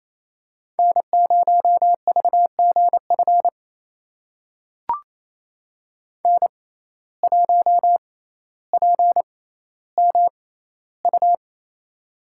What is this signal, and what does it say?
Morse code, heard as a keyed tone, about 28 wpm, spaced as for 14 wpm, 700 Hz. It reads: N0VGF N1PMU
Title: USA Call Signs - 5 Characters - 6x Character Spacing - 28wpm